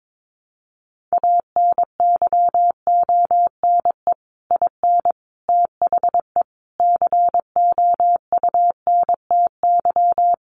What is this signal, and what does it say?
Morse code, heard as a keyed tone, about 22 wpm, 700 Hz. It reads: ANYONE IN THE COUNTY
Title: Sentences from Top 1200 Words 22wpm